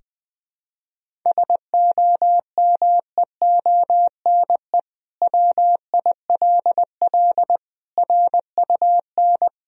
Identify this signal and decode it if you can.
Morse code, heard as a keyed tone, about 20 wpm, 700 Hz. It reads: SOMEONE WILL RUN
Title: Sentences from Top 1000 Words - Review Part 1 - 20wpm